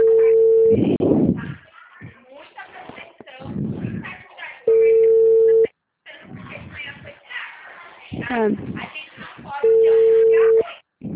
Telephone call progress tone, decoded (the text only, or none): ringback tone